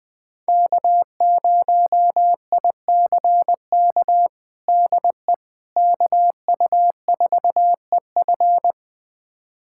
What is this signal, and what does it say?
Morse code, heard as a keyed tone, about 20 wpm, 700 Hz. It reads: K0ICK DE KU4EF
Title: USA Callsign DE Callsign - 5 Characters - 20wpm